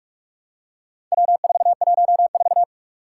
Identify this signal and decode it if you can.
Morse code, heard as a keyed tone, about 44 wpm, 700 Hz. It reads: W414
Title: Four Challenging Letter-Number Combinations 45wpm